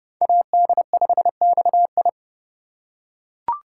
Morse code, heard as a keyed tone, about 28 wpm, 700 Hz. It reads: AD5XI